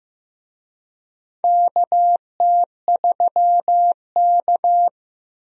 Morse code, heard as a keyed tone, about 15 wpm, 700 Hz. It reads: KT3K
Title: USA Call Signs - 4 Characters - 15wpm